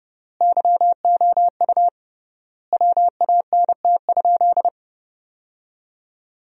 Morse code, heard as a keyed tone, about 30 wpm, 700 Hz. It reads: YOU WANT?
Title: 3x Word-Spacing - Sentences from Top 200 Words - 30wpm